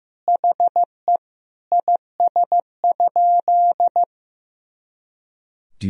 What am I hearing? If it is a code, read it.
Morse code, heard as a keyed tone, about 15 wpm, 700 Hz. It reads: HE IS?
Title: Sentences from Top 300 Words and Abbreviations 15wpm